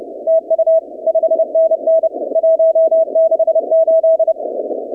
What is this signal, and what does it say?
Morse code, heard as a keyed tone, about 30 wpm, 650 Hz, with radio noise. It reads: TU 5NN 168
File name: 2019-cw-wpx